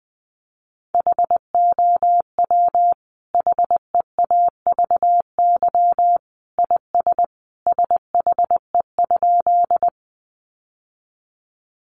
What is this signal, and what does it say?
Morse code, heard as a keyed tone, about 20 wpm, 700 Hz. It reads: HOW HEAVY IS SHE?
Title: Sentences from Top 1100 Words 20wpm